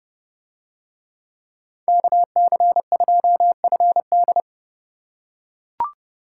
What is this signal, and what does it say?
Morse code, heard as a keyed tone, about 29 wpm, 700 Hz. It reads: KC2FD